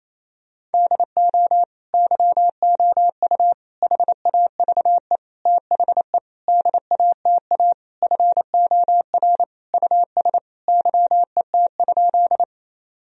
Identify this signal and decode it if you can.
Morse code, heard as a keyed tone, about 28 wpm, 700 Hz. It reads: DO YOU HAVE THE DATA FOR US YET?
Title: Sentences from Top 1000 Words - Review Part 3 - 28wpm